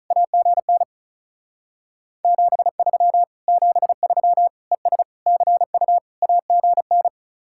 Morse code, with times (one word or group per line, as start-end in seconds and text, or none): 0.1-0.8 AGN
2.2-3.2 73
3.5-4.5 73
4.7-5.0 ES
5.3-6.0 CU
6.2-7.1 AGN